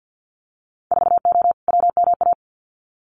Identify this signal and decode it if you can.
Morse code, heard as a keyed tone, about 50 wpm, 700 Hz. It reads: 4O WMA
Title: Winter Field Day Exchanges Encore 50wpm